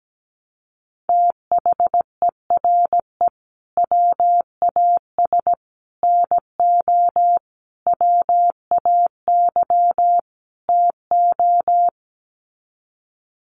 Morse code, heard as a keyed tone, about 17 wpm, 700 Hz. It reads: THERE WAS NO WAY TO
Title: Sets of 5 Words Encore 17wpm